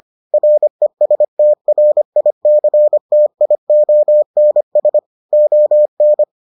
Morse code, heard as a keyed tone, about 25 wpm, 600 Hz. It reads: RESTRICTIONS ON